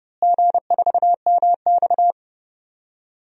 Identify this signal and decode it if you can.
Morse code, heard as a keyed tone, about 32 wpm, 700 Hz. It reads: G4MX
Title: DXCC-100 Call Signs - 4 Characters - Repeated 3x - 30wpm